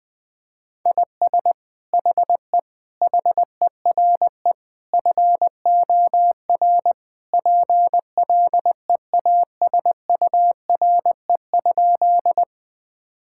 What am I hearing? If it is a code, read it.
Morse code, heard as a keyed tone, about 20 wpm, 700 Hz. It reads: IS HE HERE FOR PLEASURE?
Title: Sentences from Top 1600 Words 20wpm